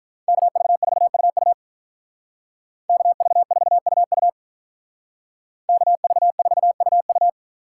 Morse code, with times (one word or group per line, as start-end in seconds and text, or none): 0.3-1.5 KV4UU
2.9-4.3 KV4UU
5.7-7.3 KV4UU